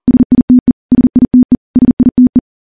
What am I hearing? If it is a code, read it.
Morse code, heard as a keyed tone, about 41 wpm, 250 Hz. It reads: SITE SITE SITE
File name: morse-training-40WPM